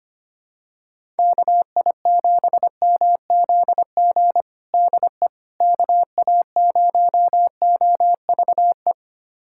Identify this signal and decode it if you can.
Morse code, heard as a keyed tone, about 25 wpm, 700 Hz. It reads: KI7MZG DE KA0OVE
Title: USA Callsign DE Callsign 25wpm